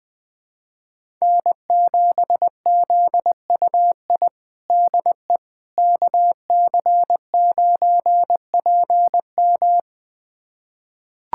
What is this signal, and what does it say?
Morse code, heard as a keyed tone, about 20 wpm, 700 Hz. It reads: N7ZUI DE KC9PM
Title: USA Callsign DE Callsign - 5 Characters - 20wpm